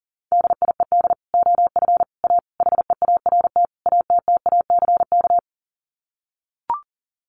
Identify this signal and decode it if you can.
Morse code, heard as a keyed tone, about 40 wpm, 700 Hz. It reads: DIED OF A HEART ATTACK
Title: Sets of 5 Words Encore2 40wpm